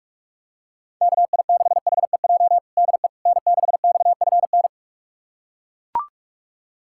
Morse code, heard as a keyed tone, about 41 wpm, 700 Hz. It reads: KI6HEJ DE N6XFN
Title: USA Callsign DE Callsign 45wpm